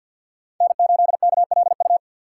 Morse code, heard as a keyed tone, about 51 wpm, 690 Hz. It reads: N8KLU
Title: USA Call Signs - 2x Character Spacing - 50wpm